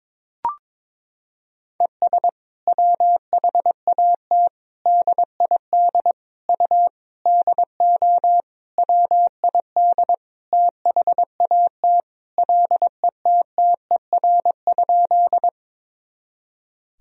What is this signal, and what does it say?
Morse code, heard as a keyed tone, about 22 wpm, 700 Hz. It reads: ES WHAT DID U DO WID THAT LETTER?
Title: Sentences from Top 300 Words and Abbreviations 22wpm